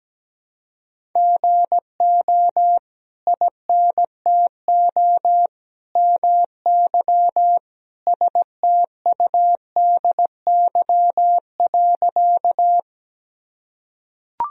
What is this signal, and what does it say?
Morse code, heard as a keyed tone, about 17 wpm, 700 Hz. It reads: GO INTO MY STUDY.
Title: Sentences from Top 1000 Words - Review Part 1 - 17wpm